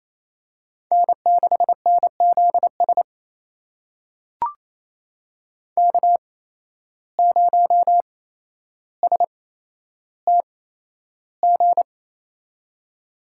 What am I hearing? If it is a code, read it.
Morse code, heard as a keyed tone, about 28 wpm, spaced as for 12 wpm, 700 Hz. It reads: N6NZS K0STG
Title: USA Call Signs - 5 Characters - 8x Character Spacing - 28wpm